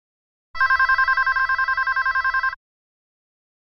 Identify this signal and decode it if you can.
There is no code.